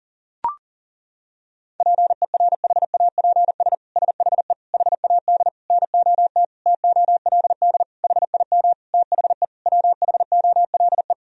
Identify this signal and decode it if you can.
Morse code, heard as a keyed tone, about 40 wpm, 700 Hz. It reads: PERHAPS SHE HAD NOT TOLD HIM THE WHOLE
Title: Sentences from Top 1000 Words - Review Part 1 - 40wpm